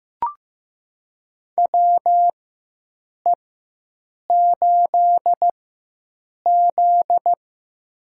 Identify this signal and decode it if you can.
Morse code, heard as a keyed tone, about 15 wpm, spaced as for 7 wpm, 700 Hz. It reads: WE8Z
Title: USA Call Signs - 4 Characters - 4x Character Spacing - 15wpm